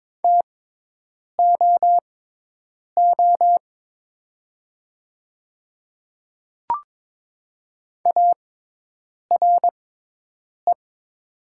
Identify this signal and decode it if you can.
Morse code, heard as a keyed tone, about 22 wpm, spaced as for 8 wpm, 700 Hz. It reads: TOO ARE
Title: 6x Character Spacing - Top 200 Words - 22wpm